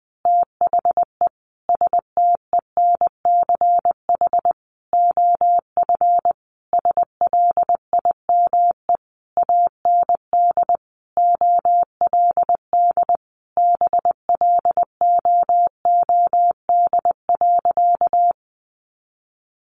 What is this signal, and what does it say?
Morse code, heard as a keyed tone, about 20 wpm, 700 Hz. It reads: THE STENCH OF SLIME AND OLD BLOOD.
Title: A Sound Of Thunder 20wpm